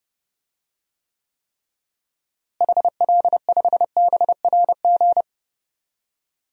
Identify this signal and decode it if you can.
Morse code, heard as a keyed tone, about 30 wpm, 700 Hz. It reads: HL5BRG
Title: DXCC-100 Call Signs - 6 Characters - Repeated 3x - 30wpm